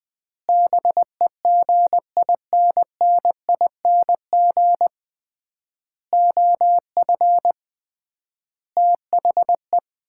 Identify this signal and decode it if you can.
Morse code, heard as a keyed tone, about 20 wpm, 700 Hz. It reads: BEGINNING OF THE
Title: 3x Word-Spacing - Sets of 5 Words - 20wpm